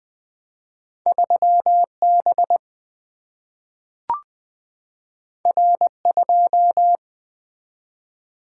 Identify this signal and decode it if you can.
Morse code, heard as a keyed tone, about 20 wpm, 700 Hz. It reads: 3B R2
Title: Single-Letter Single-Number 20wpm